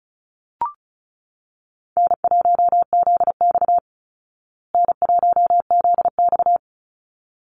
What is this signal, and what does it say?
Morse code, heard as a keyed tone, about 34 wpm, 700 Hz. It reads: N1ZX N1ZX
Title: Easy Call Signs Repeated 2x - 35wpm